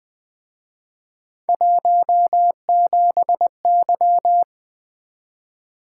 Morse code, heard as a keyed tone, about 23 wpm, 700 Hz. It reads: 17Y
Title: Speed-Racing - Three Letter-Number Combinations - 20wpm